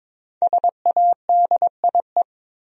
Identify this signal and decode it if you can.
Morse code, heard as a keyed tone, about 22 wpm, 700 Hz. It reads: SADIE
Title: USA Common Names - Max 5 Letters - 22wpm